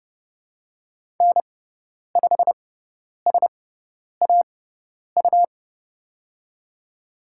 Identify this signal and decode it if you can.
Morse code, heard as a keyed tone, about 30 wpm, spaced as for 10 wpm, 690 Hz. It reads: N5SAU
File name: Lesson 18g - Letter m - Call Signs 30wpm